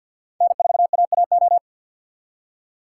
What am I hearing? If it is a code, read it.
Morse code, heard as a keyed tone, about 49 wpm, 700 Hz. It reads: N4AAO
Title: USA Call Signs - 5 Characters - 4x Character Spacing - 50wpm